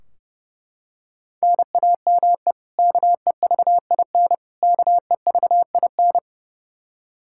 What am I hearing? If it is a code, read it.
Morse code, heard as a keyed tone, about 30 wpm, 700 Hz. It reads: NAME KEVIN KEVIN
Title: Practice QSO - Exchange 1 - Beginning - 30wpm